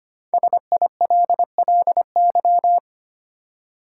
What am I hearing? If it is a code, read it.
Morse code, heard as a keyed tone, about 25 wpm, 700 Hz. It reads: SILLY
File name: 5 Letter Words - First 3 Letters Given - 25wpm